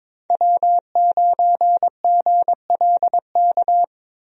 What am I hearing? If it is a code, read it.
Morse code, heard as a keyed tone, about 22 wpm, 700 Hz. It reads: W9GLK
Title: USA Call Signs - 5 Characters - Repeated 2x - 22wpm